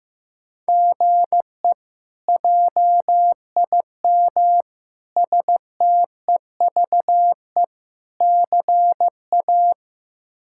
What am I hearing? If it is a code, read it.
Morse code, heard as a keyed tone, about 15 wpm, 700 Hz. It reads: GE JIM STEVE CA
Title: K1USN SST QSO Practice - 15wpm